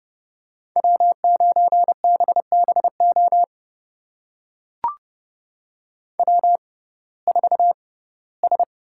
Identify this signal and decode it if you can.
Morse code, heard as a keyed tone, about 30 wpm, spaced as for 18 wpm, 700 Hz. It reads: W9BBO W4S
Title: USA Call Signs - 5 Characters - 6x Character Spacing - 30wpm